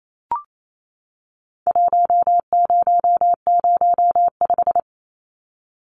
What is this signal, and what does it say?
Morse code, heard as a keyed tone, about 28 wpm, 700 Hz. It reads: 1005
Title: Numbers - Contest Style - 4-Digit 28wpm